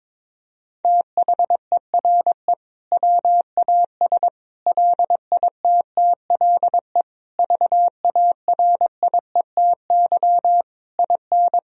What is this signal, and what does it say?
Morse code, heard as a keyed tone, about 22 wpm, 700 Hz. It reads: THERE WAS LITTLE VARIETY IN
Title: Sentences from Top 1000 Words - Review Part 4 - 22wpm